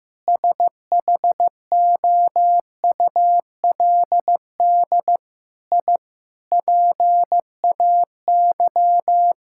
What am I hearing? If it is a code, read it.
Morse code, heard as a keyed tone, about 16 wpm, 700 Hz. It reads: SHOULD I PAY
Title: Sentences from Top 1000 Words - Review Part 5 - 15wpm